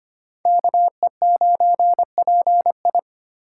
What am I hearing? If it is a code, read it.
Morse code, heard as a keyed tone, about 32 wpm, 700 Hz. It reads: KE9PI